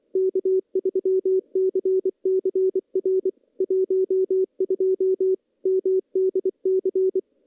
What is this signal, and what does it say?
Morse code, heard as a keyed tone, about 23 wpm, 380 Hz. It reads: K3CCR 12 MDC
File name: CW-SS-k5tr